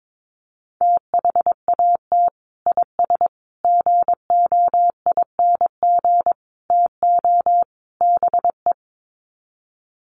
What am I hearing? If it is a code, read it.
Morse code, heard as a keyed tone, about 22 wpm, 700 Hz. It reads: THAT IS GOING TO BE
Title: Sets of 5 Words Encore2 22wpm